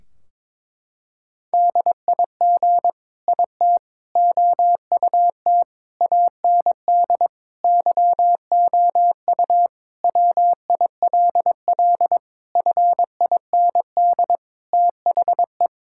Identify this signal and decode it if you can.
Morse code, heard as a keyed tone, about 22 wpm, 700 Hz. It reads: DIG IT OUT AND YOU WILL FIND THE
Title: Sentences from Top 1600 Words 22wpm